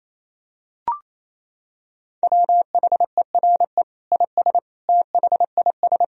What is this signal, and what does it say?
Morse code, heard as a keyed tone, about 28 wpm, 700 Hz. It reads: WHERE IS THIS
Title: Sentences from Top 1000 Words - Review Part 3 - 28wpm